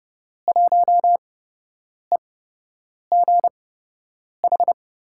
Morse code, heard as a keyed tone, about 30 wpm, spaced as for 8 wpm, 700 Hz. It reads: 1EGH
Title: USA Call Signs - 5 Characters - 8x Character Spacing - 30wpm